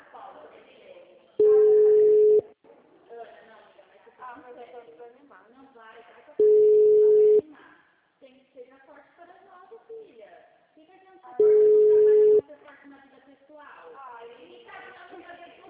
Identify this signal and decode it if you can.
Ringback tone (Europe, 425 Hz), with speech.